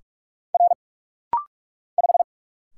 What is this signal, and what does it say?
Morse code, heard as a keyed tone, about 46 wpm, 700 Hz. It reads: R 5
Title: Single Letter-Number Alt - Rapid-Fire 45wpm